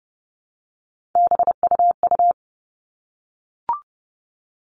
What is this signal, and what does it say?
Morse code, heard as a keyed tone, about 30 wpm, 700 Hz. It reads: BUU
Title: All Cut Numbers - 3-Digit 30wpm